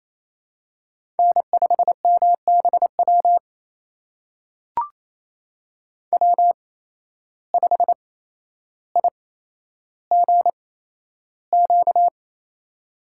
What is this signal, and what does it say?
Morse code, heard as a keyed tone, about 27 wpm, spaced as for 12 wpm, 700 Hz. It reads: N5MBW W5IGQ
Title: USA Call Signs - 5 Characters - 8x Character Spacing - 28wpm